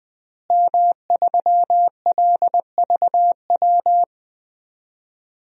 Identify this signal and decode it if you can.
Morse code, heard as a keyed tone, about 21 wpm, 700 Hz. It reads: M3LVW
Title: DXCC-100 Call Signs - 5 Characters - Repeated 3x - 20wpm